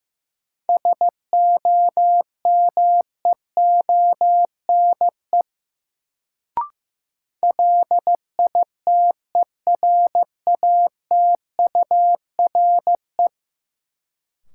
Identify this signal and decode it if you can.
Morse code, heard as a keyed tone, about 15 wpm, 700 Hz. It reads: SOMEONE LITERATURE
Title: Sentences from Top 600 Words - New Word Drill - 15wpm